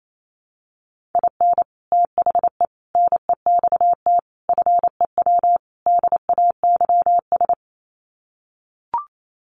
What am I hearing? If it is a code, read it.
Morse code, heard as a keyed tone, about 28 wpm, 700 Hz. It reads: IN THE NEXT FEW DAYS
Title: Sets of 5 Words Encore2 28wpm